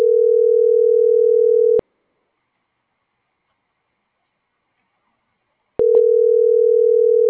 Ringback tone (North America), 440 + 480 Hz.